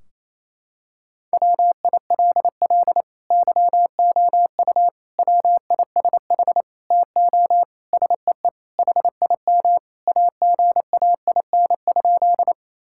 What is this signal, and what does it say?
Morse code, heard as a keyed tone, about 28 wpm, 700 Hz. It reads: WILL YOU WISH TO SEE HIM AGAIN?
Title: Sentences from Top 1600 Words 28wpm